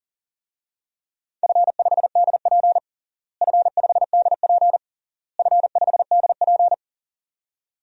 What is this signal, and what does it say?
Morse code, heard as a keyed tone, about 40 wpm, 700 Hz. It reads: F5DP F5DP F5DP
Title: DXCC-100 Call Signs - 4 Characters - Repeated 3x - 40wpm